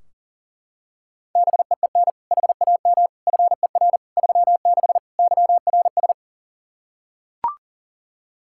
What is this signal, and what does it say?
Morse code, heard as a keyed tone, about 40 wpm, 700 Hz. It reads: BEEN HAM FER 36 YRS
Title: Practice QSO - Exchange 3 - More Info - 40wpm